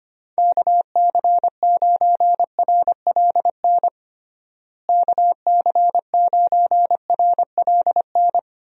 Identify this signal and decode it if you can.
Morse code, heard as a keyed tone, about 25 wpm, 700 Hz. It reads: KC9RLN KC9RLN
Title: USA Call Signs - 6 Characters - Repeated 2x - 25wpm